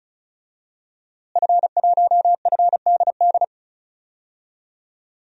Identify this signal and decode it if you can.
Morse code, heard as a keyed tone, about 36 wpm, 700 Hz. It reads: F1FDD
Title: DXCC-100 Call Signs - 5 Characters - Repeated 2x - 35wpm